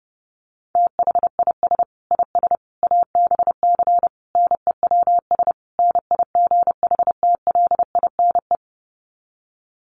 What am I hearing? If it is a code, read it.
Morse code, heard as a keyed tone, about 30 wpm, 700 Hz. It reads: THIS IS ABC NEWS NIGHTLINE
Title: Sets of 5 Words 30wpm